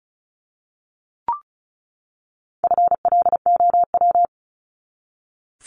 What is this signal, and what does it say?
Morse code, heard as a keyed tone, about 34 wpm, 700 Hz. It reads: FLOW
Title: Top 1001-1250 Words 35wpm